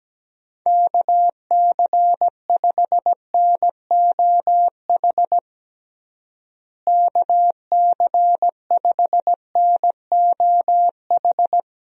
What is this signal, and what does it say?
Morse code, heard as a keyed tone, about 17 wpm, 700 Hz. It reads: KC5NOH KC5NOH
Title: USA Call Signs - 6 Characters - Repeated 3x - 17wpm